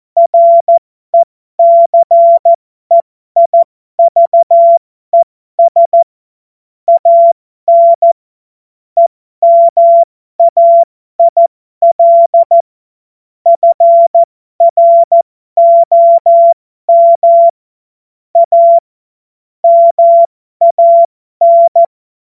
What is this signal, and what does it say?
Morse code, heard as a keyed tone, about 14 wpm, spaced as for 12 wpm, 680 Hz. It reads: RECEIVES AN EMAIL FROM A MAN
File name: Morsekurs / Witze